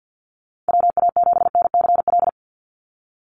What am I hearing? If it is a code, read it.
Morse code, heard as a keyed tone, about 50 wpm, 700 Hz. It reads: WA7NCL
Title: NA SOTA Activator Callsigns 50wpm